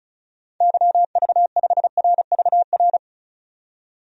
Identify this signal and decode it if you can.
Morse code, heard as a keyed tone, about 36 wpm, 700 Hz. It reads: YV5RVR
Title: DXCC-100 Call Signs - 6 Characters - 35wpm